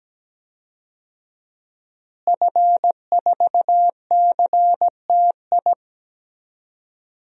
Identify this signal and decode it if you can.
Morse code, heard as a keyed tone, about 17 wpm, 700 Hz. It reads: F4CTI
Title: DXCC-100 Call Signs - 5 Characters - Repeated 3x - 17wpm